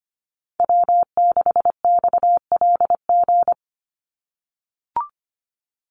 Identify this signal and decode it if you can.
Morse code, heard as a keyed tone, about 25 wpm, 700 Hz. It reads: W6XLG
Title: USA Call Signs 25wpm